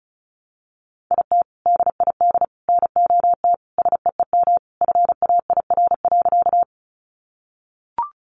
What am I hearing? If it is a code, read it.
Morse code, heard as a keyed tone, about 34 wpm, 700 Hz. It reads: IT DID NOT SEEM FAIR.
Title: Sentences from Top 1300 Words 35wpm